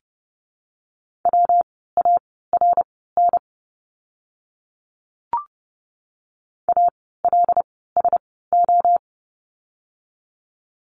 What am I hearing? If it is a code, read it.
Morse code, heard as a keyed tone, about 30 wpm, spaced as for 17 wpm, 700 Hz. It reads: WARN ALSO
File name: course / 30:10wpm / 115 Lesson 19f - Letter w - Words 30wpm